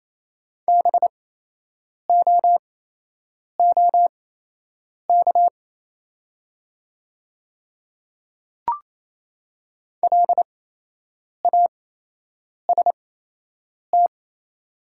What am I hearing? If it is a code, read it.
Morse code, heard as a keyed tone, about 28 wpm, spaced as for 8 wpm, 700 Hz. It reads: BOOK LAST
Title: 8x Character Spacing - Top 500 Words - 28wpm